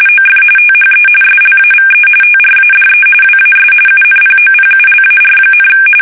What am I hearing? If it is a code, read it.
RTTY, about 45.45 baud. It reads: OF THE SLAVE TRADE WAS UNPOPULAR,
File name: fsk850 / usnavy2